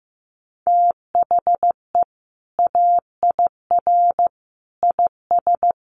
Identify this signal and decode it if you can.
Morse code, heard as a keyed tone, about 15 wpm, 700 Hz. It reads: THE AIR IS
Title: Sentences from Top 1500 Words 15wpm